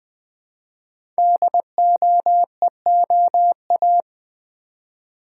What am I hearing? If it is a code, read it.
Morse code, heard as a keyed tone, about 20 wpm, 700 Hz. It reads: DOEOA
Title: Code Groups - 5 Letters - 20wpm